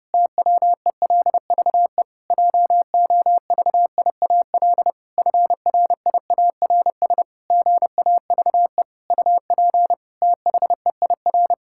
Morse code, heard as a keyed tone, about 30 wpm, 700 Hz. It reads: TWELVE JOVIAL FRIARS GAVE UP THEIR
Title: Pangram Challenge v2 - 30wpm